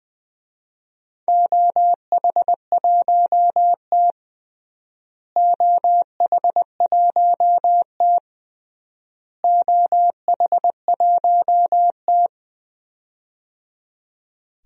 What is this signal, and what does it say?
Morse code, heard as a keyed tone, about 20 wpm, 700 Hz. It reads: OH1T OH1T OH1T
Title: DXCC-100 Call Signs - 4 Characters - Repeated 3x - 20wpm